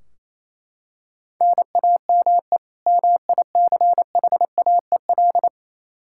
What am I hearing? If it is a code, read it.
Morse code, heard as a keyed tone, about 28 wpm, 700 Hz. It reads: NAME MICHAEL
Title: Practice QSO - Exchange 1 - Beginning - 28wpm